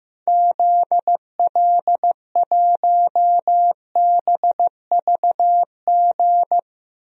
Morse code, heard as a keyed tone, about 15 wpm, 700 Hz. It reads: ZL1BVG
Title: DXCC-100 Call Signs - 6 Characters - Repeated 3x - 15wpm